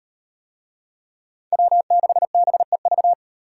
Morse code, heard as a keyed tone, about 38 wpm, 700 Hz. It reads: W6BEV